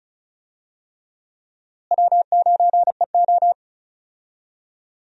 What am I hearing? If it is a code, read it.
Morse code, heard as a keyed tone, about 36 wpm, 700 Hz. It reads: W9EO
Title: USA Call Signs - 4 Characters - Repeated 3x - 35wpm